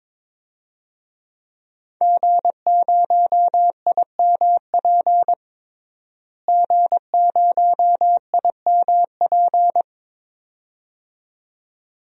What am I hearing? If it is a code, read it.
Morse code, heard as a keyed tone, about 22 wpm, 700 Hz. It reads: G0IMP G0IMP
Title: DXCC-100 Call Signs - 5 Characters - Repeated 2x - 22wpm